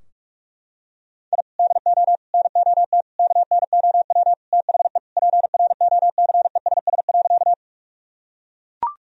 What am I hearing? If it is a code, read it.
Morse code, heard as a keyed tone, about 41 wpm, 700 Hz. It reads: I DO NOT KNOW THE PROCESS.
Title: Sentences from Top 1000 Words - Review Part 3 - 45wpm